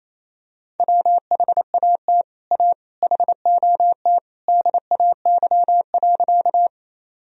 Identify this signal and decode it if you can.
Morse code, heard as a keyed tone, about 28 wpm, 700 Hz. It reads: WHAT A HOT DAY.
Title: Sentences from Top 1000 Words - Review Part 3 - 28wpm